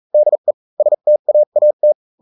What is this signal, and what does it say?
Morse code, heard as a keyed tone, about 40 wpm, spaced as for 35 wpm, 600 Hz. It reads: DE STAAT